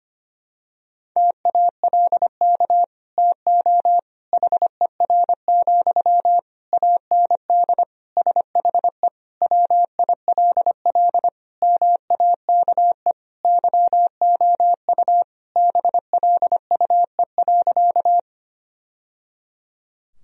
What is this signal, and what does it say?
Morse code, heard as a keyed tone, about 25 wpm, 700 Hz. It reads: TALK TO HER, AND SHE WILL MAKE YOU BLUE.
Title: Sentences from Top 1000 Words - Review Part 3 - 25wpm